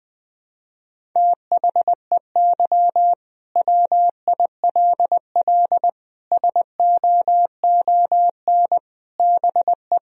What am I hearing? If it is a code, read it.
Morse code, heard as a keyed tone, about 20 wpm, 700 Hz. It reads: THEY WILL SOON BE
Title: Sentences from Top 1000 Words - Review Part 4 - 20wpm